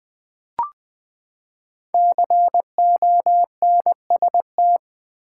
Morse code, heard as a keyed tone, about 20 wpm, 700 Hz. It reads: CONST
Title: Build A Word - 5 Letters At A Time - 12-Letter Words - 20wpm